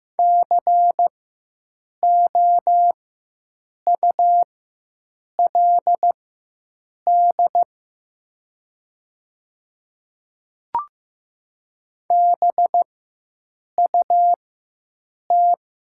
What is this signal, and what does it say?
Morse code, heard as a keyed tone, about 15 wpm, spaced as for 7 wpm, 700 Hz. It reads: COULD BUT
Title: 4x Character Spacing - Top 100 Words - 15wpm